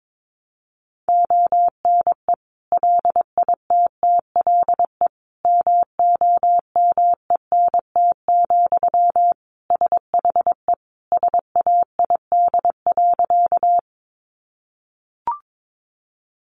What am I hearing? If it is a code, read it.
Morse code, heard as a keyed tone, about 22 wpm, 700 Hz. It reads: ONE LITTLE MOMENT, SHE SAID.
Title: Sentences from Top 1000 Words 22wpm